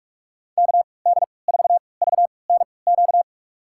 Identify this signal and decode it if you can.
Morse code, heard as a keyed tone, about 41 wpm, spaced as for 25 wpm, 700 Hz. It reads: KD4VNQ
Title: USA Call Signs - 3x Character Spacing - 45wpm